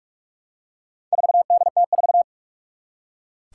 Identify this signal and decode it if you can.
Morse code, heard as a keyed tone, about 44 wpm, 700 Hz. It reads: 4DT4
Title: Four Challenging Letter-Number Combinations 45wpm